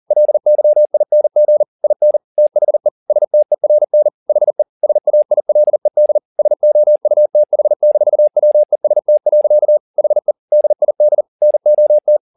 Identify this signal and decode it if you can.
Morse code, heard as a keyed tone, about 40 wpm, 600 Hz. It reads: LYING IN THE STERN HE SAILED SOUTH-WEST. HE DID NOT
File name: the old man and the sea 2 40wpm